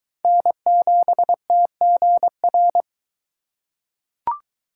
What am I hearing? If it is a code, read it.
Morse code, heard as a keyed tone, about 22 wpm, 700 Hz. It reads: N7TGR